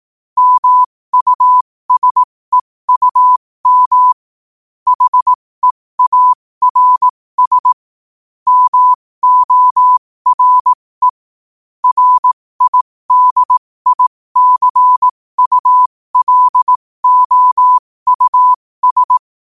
Morse code, heard as a keyed tone, about 18 wpm, spaced as for 15 wpm, 1000 Hz. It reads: MUSEUM HEARS MORE RIDICULOUS
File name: qotdfiles / 15wpm